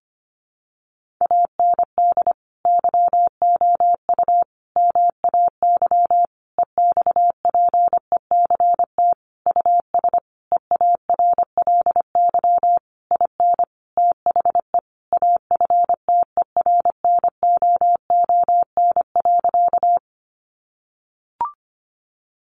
Morse code, heard as a keyed tone, about 25 wpm, 700 Hz. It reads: AND YOU MAY EXPECT US EARLY IN THE AFTERNOON.
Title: Sentences from Top 1500 Words 25wpm